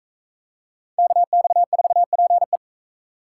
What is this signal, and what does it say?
Morse code, heard as a keyed tone, about 40 wpm, 700 Hz. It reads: KX4PE